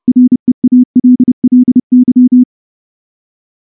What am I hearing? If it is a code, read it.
Morse code, heard as a keyed tone, about 28 wpm, 250 Hz. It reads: REALLY
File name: morse / morse-training-30WPM